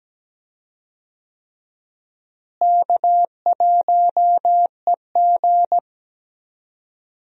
Morse code, heard as a keyed tone, about 17 wpm, 700 Hz. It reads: K1EG
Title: USA Call Signs - 4 Characters - Repeated 3x - 17wpm